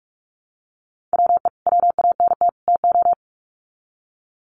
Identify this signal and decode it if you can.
Morse code, heard as a keyed tone, about 44 wpm, 700 Hz. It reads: WE WANT TO
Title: Sets of 3 Words 45wpm